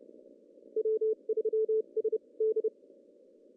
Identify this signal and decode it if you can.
Morse code, heard as a keyed tone, about 30 wpm, spaced as for 25 wpm, 440 Hz. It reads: W3SD
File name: CW / cqww-cw-k5tr